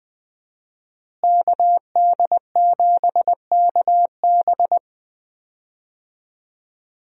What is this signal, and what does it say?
Morse code, heard as a keyed tone, about 20 wpm, 700 Hz. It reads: KD7KB
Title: USA Call Signs - 5 Characters - Repeated 2x - 20wpm